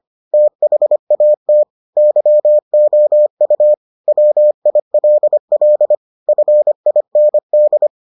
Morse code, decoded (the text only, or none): THAT YOU WILL FIND